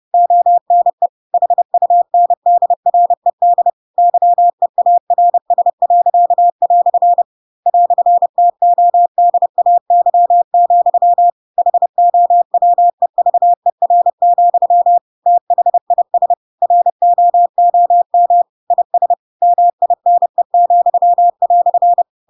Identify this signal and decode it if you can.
Morse code, heard as a keyed tone, about 30 wpm, 700 Hz. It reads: ONE HUNDRED YEARS." "TODAY, HOWEVER, THIS ROOM IS MINE,"
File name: Pinocchio / Pinocchio ch4 30 30 ik1hge